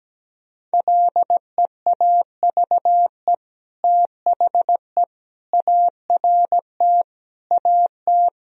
Morse code, heard as a keyed tone, about 17 wpm, 700 Hz. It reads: LEAVE THE ART AT
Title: Sentences from Top 1000 Words - Review Part 3 - 17wpm